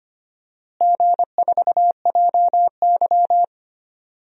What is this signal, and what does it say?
Morse code, heard as a keyed tone, about 25 wpm, 700 Hz. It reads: G4JY
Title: DXCC-100 Call Signs - 4 Characters - Repeated 3x - 25wpm